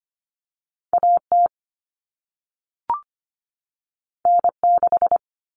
Morse code, heard as a keyed tone, about 25 wpm, 700 Hz. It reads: AT N6
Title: Common Cut Numbers - 2-Digit 25wpm